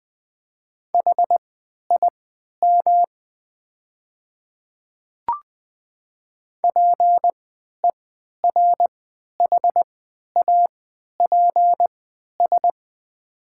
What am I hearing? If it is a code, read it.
Morse code, heard as a keyed tone, about 20 wpm, spaced as for 11 wpm, 700 Hz. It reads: HIM PERHAPS
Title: 3x Character Spacing - Top 500 Words - 20wpm